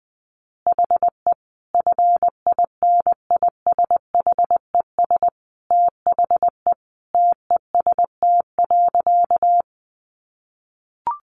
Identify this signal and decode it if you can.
Morse code, heard as a keyed tone, about 20 wpm, 700 Hz. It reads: HE FINISHES THE TEST.
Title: Additional Easy Phrases 20wpm